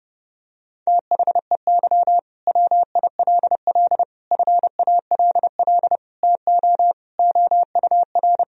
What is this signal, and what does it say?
Morse code, heard as a keyed tone, about 30 wpm, 700 Hz. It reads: THEY WILL FALL TO OUR
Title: Sentences from Top 1000 Words - Review Part 4 - 30wpm